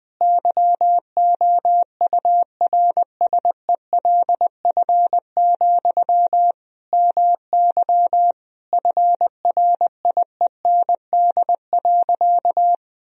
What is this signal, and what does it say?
Morse code, heard as a keyed tone, about 20 wpm, 700 Hz. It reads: YOURSELF, MY FRIEND.
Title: Sentences from Top 1000 Words - Review Part 3 - 20wpm